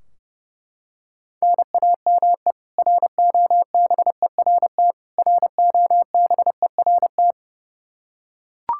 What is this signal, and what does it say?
Morse code, heard as a keyed tone, about 32 wpm, 700 Hz. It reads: NAME ROBERT ROBERT
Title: Practice QSO - Exchange 1 - Beginning - 30wpm